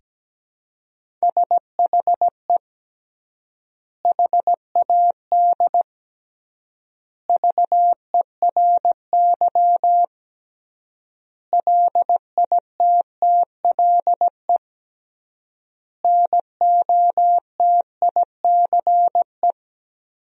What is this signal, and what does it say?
Morse code, heard as a keyed tone, about 17 wpm, 700 Hz. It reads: SHE HAD VERY LITTLE NOTICE
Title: 3x Word-Spacing - Sentences from Top 400 Words - 17wpm